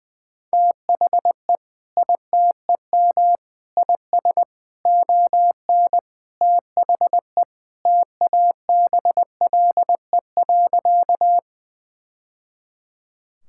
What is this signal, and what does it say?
Morse code, heard as a keyed tone, about 20 wpm, 700 Hz. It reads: THE ITEM IS ON THE TABLE.
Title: Sentences from Top 1000 Words - Review Part 6 - 20wpm